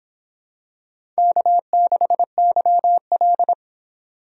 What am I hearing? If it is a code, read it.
Morse code, heard as a keyed tone, about 26 wpm, 700 Hz. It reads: K6YL